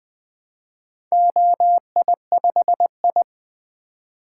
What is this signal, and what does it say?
Morse code, heard as a keyed tone, about 20 wpm, 700 Hz. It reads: OI5I
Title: Four Letter-Number Combinations - 20wpm